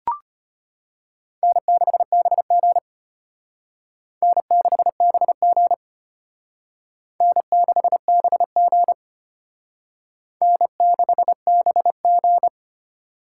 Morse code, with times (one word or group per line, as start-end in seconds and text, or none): 1.4-2.8 N6BG
4.2-5.7 N6BG
7.2-8.9 N6BG
10.4-12.5 N6BG